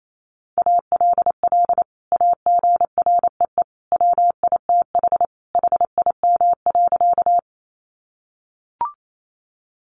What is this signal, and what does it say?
Morse code, heard as a keyed tone, about 28 wpm, 700 Hz. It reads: ALL AGREE WITH HIM.
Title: Sentences from Top 1100 Words 28wpm